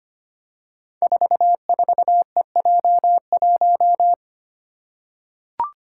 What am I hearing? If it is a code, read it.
Morse code, heard as a keyed tone, about 25 wpm, 700 Hz. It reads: 44EJ1
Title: Code Groups - 5 Letter:Numbers - 25wpm